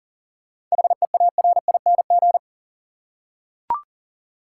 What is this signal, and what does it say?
Morse code, heard as a keyed tone, about 40 wpm, 700 Hz. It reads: HEARING
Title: Seven Letter Words - 40wpm